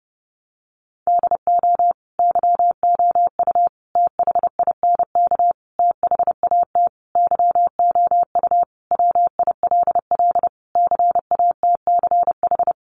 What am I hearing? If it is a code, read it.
Morse code, heard as a keyed tone, about 30 wpm, 700 Hz. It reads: DO YOU THINK THAT YOU WILL CATCH
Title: Sentences from Top 1500 Words 30wpm